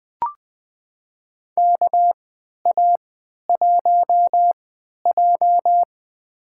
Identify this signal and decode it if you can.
Morse code, heard as a keyed tone, about 20 wpm, spaced as for 11 wpm, 700 Hz. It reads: KA1J
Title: USA Call Signs - 3x Character Spacing - 20wpm